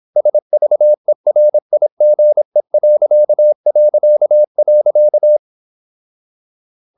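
Morse code, heard as a keyed tone, about 26 wpm, 600 Hz. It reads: SVERIGE...
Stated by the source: High-speed (QRQ) Morse at 26 wpm.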